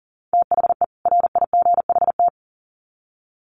Morse code, heard as a keyed tone, about 40 wpm, 700 Hz. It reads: THE RIGHT
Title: Sets of 2 Words 40wpm